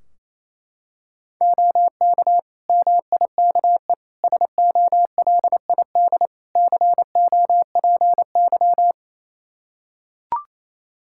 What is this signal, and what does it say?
Morse code, heard as a keyed tone, about 28 wpm, 700 Hz. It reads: OK MIKE SOLID COPY
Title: Practice QSO - Exchange 3 - More Info - 28wpm